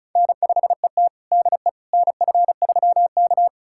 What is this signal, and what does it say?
Morse code, heard as a keyed tone, about 36 wpm, 700 Hz. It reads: N5ET DE NF3K
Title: USA Callsign DE Callsign - 4 Characters - 35wpm